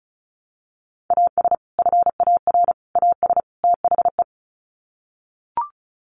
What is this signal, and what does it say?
Morse code, heard as a keyed tone, about 34 wpm, 700 Hz. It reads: AS FAR AS THE
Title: Sets of 4 Words 35wpm